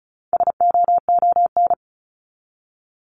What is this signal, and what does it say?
Morse code, heard as a keyed tone, about 34 wpm, 700 Hz. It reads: SOON
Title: Top 500 Words 35wpm